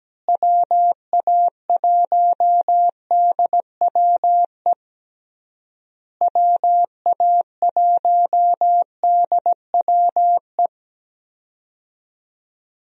Morse code, heard as a keyed tone, about 17 wpm, 700 Hz. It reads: WA1DWE WA1DWE
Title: USA Call Signs - 6 Characters - Repeated 3x - 17wpm